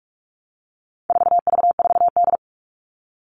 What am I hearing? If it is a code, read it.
Morse code, heard as a keyed tone, about 44 wpm, 700 Hz. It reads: 4V4D
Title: All Cut Numbers - 4-Digit 45wpm